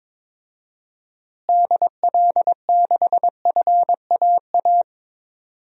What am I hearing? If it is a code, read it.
Morse code, heard as a keyed tone, about 22 wpm, 700 Hz. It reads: DL6FAA
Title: DXCC-100 Call Signs - 6 Characters - Repeated 3x - 22wpm